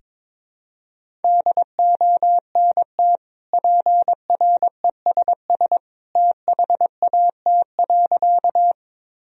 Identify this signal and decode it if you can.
Morse code, heard as a keyed tone, about 22 wpm, 700 Hz. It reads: DONT PRESS THAT.
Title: Sentences from Top 1000 Words - Review Part 1 - 22wpm